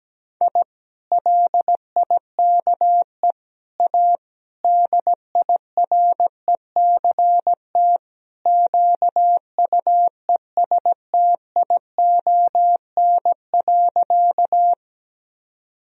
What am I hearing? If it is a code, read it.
Morse code, heard as a keyed tone, about 17 wpm, 700 Hz. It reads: I LIKE A DIRECT QUESTION.
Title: Sentences from Top 1000 Words - Review Part 2 - 17wpm